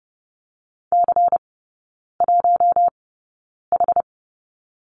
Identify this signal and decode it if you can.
Morse code, heard as a keyed tone, about 30 wpm, spaced as for 9 wpm, 700 Hz. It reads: C1H
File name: Lesson 17e - Letter u - Three Characters 30wpm